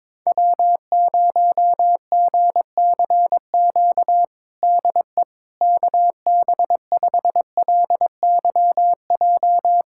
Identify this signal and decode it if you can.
Morse code, heard as a keyed tone, about 21 wpm, 700 Hz. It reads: W0GCQ DE KB5LYJ
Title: USA Callsign DE Callsign 22wpm